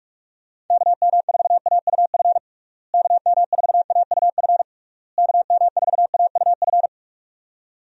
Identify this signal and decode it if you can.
Morse code, heard as a keyed tone, about 42 wpm, 700 Hz. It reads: KM4AUF KM4AUF KM4AUF
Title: USA Call Signs - 6 Characters - Repeated 3x - 45wpm